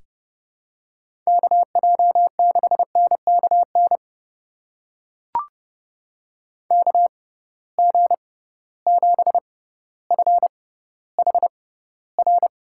Morse code, heard as a keyed tone, about 30 wpm, spaced as for 15 wpm, 700 Hz. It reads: KJ6NKN KG7FHR
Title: USA Call Signs - 6x Character Spacing - 30wpm